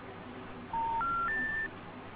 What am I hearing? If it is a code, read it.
Special information tone, with noise.
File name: recordings / REfe7c39de656ea80b6adf0991d1ec0560